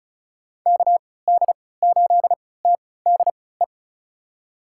Morse code, heard as a keyed tone, about 36 wpm, spaced as for 20 wpm, 700 Hz. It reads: KD8TDE
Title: USA Call Signs - 6 Characters - 3x Character Spacing - 35wpm